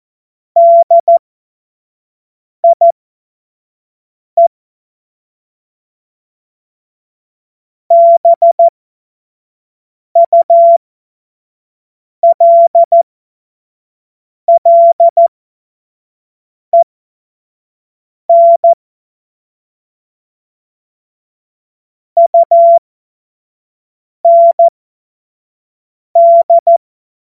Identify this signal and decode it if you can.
Morse code, heard as a keyed tone, about 14 wpm, spaced as for 5 wpm, 680 Hz. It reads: DIE BULLEN UND
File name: Morsekurs / Witze / Witz 30. August 2022 25 F70